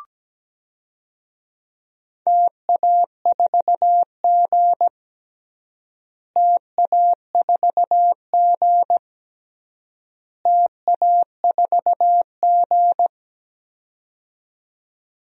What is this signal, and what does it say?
Morse code, heard as a keyed tone, about 17 wpm, 700 Hz. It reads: TA4G TA4G TA4G
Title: DXCC-100 Call Signs - 4 Characters - Repeated 3x - 17wpm